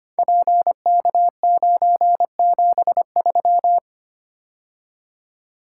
Morse code, heard as a keyed tone, about 25 wpm, 700 Hz. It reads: PK973
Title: Code Groups - 5 Letter:Numbers - 25wpm